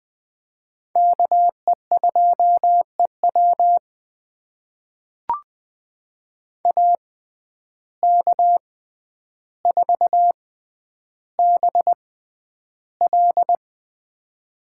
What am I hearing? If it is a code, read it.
Morse code, heard as a keyed tone, about 20 wpm, spaced as for 10 wpm, 700 Hz. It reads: KE2EW AK4BL
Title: USA Call Signs - 5 Characters - 6x Character Spacing - 20wpm